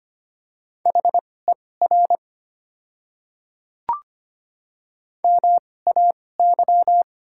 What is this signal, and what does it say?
Morse code, heard as a keyed tone, about 25 wpm, spaced as for 18 wpm, 700 Hz. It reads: HER MAY
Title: 2x Character Spacing - Top 100 Words - 25wpm